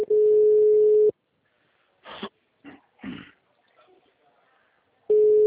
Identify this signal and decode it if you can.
Ringback tone (Europe, 425 Hz), with speech.